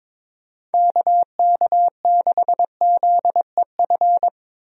Morse code, heard as a keyed tone, about 21 wpm, 700 Hz. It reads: KK6ZEF
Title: USA Call Signs - 6 Characters - Repeated 3x - 22wpm